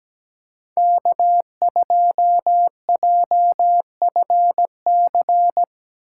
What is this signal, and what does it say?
Morse code, heard as a keyed tone, about 17 wpm, 700 Hz. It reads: K2JFC
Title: USA Call Signs - 5 Characters - 2x Character Spacing - 17wpm